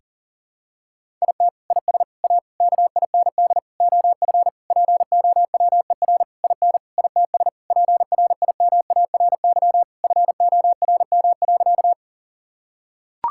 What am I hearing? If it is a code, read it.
Morse code, heard as a keyed tone, about 40 wpm, 700 Hz. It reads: IT IS A KIND OF POWER IN ITS PRIMARY FORM.
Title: Sentences from Top 1000 Words - Review Part 5 - 40wpm